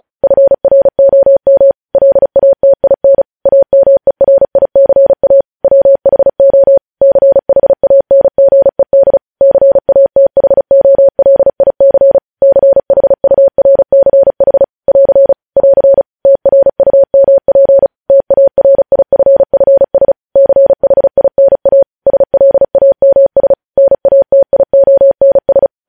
Morse code, heard as a keyed tone, about 36 wpm, 550 Hz. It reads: FROM LATIN AMERICA WHO CHANGED CATHOLIC CHURCH <AR> <AR> TRUMP TARIFFS CHINA SLAMS NATIONS
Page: News Headlines in Morse Code at 35 WPM